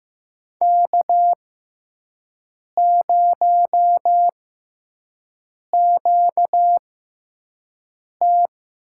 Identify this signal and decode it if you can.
Morse code, heard as a keyed tone, about 15 wpm, spaced as for 5 wpm, 700 Hz. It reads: K0QT
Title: USA Call Signs - 4 Characters - 6x Character Spacing - 15wpm